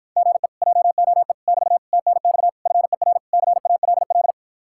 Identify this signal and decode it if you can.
Morse code, heard as a keyed tone, about 54 wpm, 700 Hz. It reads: GE JOE <BT> TNX FER CALL